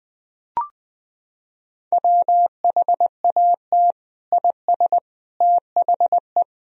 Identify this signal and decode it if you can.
Morse code, heard as a keyed tone, about 21 wpm, 700 Hz. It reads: WHAT IS THE
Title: Sentences from Top 1000 Words - Review Part 3 - 20wpm